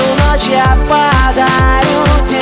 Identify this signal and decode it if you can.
There is no code.